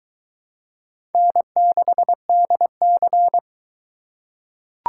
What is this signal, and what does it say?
Morse code, heard as a keyed tone, about 21 wpm, 700 Hz. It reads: N6DC